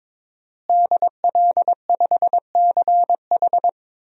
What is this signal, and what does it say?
Morse code, heard as a keyed tone, about 22 wpm, 700 Hz. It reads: DL5CH
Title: DXCC-100 Call Signs - 5 Characters - Repeated 3x - 22wpm